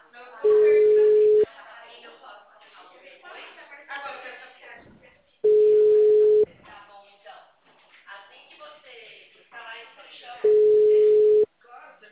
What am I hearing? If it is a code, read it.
Ringback tone (Europe, 425 Hz), with speech.
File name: out- -2011- - - .11009